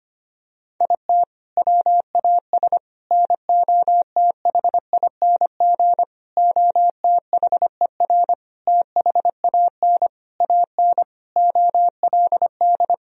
Morse code, heard as a keyed tone, about 25 wpm, 700 Hz. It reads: IT WAS NOTHING OTHER THAN AN OLD
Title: Sentences from Top 1000 Words - Review Part 2 - 25wpm